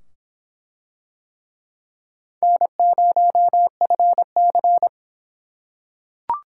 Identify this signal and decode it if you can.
Morse code, heard as a keyed tone, about 22 wpm, 700 Hz. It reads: N0FC